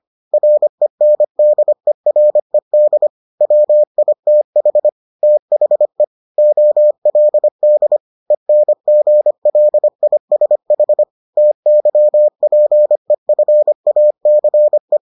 Morse code, heard as a keyed tone, about 25 wpm, 600 Hz. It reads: RENDERED WITH THE OLD ENGLISH TYPEFACE